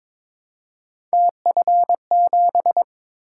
Morse code, heard as a keyed tone, about 22 wpm, 700 Hz. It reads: TF7
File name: practice / 22wpm / Three Letter-Number Combinations - 22wpm